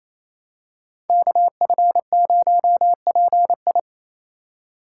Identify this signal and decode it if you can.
Morse code, heard as a keyed tone, about 28 wpm, 700 Hz. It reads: KF0PI
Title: USA Call Signs - 5 Characters - 2x Character Spacing - 28wpm